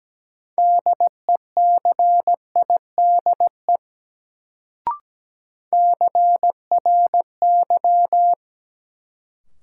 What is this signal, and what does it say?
Morse code, heard as a keyed tone, about 17 wpm, 700 Hz. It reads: DECIDE CRY
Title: Sentences from Top 500 Words - New Word Drill - 17wpm